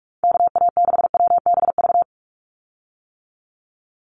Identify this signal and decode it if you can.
Morse code, heard as a keyed tone, about 44 wpm, 700 Hz. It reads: KA6WBV
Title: USA Call Signs 45wpm